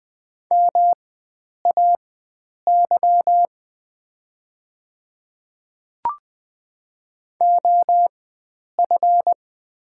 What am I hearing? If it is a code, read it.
Morse code, heard as a keyed tone, about 20 wpm, spaced as for 9 wpm, 690 Hz. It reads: MAY OF
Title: 4x Character Spacing - Top 100 Words - 20wpm